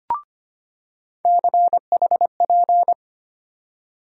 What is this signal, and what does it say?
Morse code, heard as a keyed tone, about 25 wpm, 700 Hz. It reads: CHP
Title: USA Call Sign Postfixes - 3 Letters 25wpm